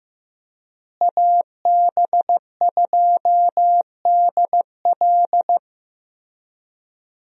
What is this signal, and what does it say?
Morse code, heard as a keyed tone, about 16 wpm, 700 Hz. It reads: AB2DL
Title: USA Call Signs - 5 Characters - Repeated 2x - 15wpm